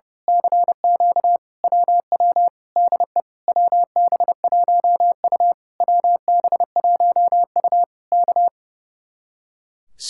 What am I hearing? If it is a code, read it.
Morse code, heard as a keyed tone, about 30 wpm, 700 Hz. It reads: CQ WW DE WB1U WB1U K
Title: Speed-Racing Common QSO Phrases - 30wpm